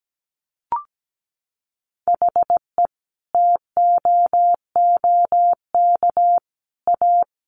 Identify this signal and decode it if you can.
Morse code, heard as a keyed tone, about 17 wpm, 700 Hz. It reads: HE TOOK A
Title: A Sound Of Thunder 17wpm